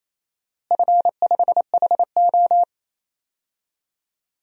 Morse code, heard as a keyed tone, about 28 wpm, 700 Hz. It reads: F5HO
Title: DXCC-100 Call Signs - 4 Characters - Repeated 2x - 28wpm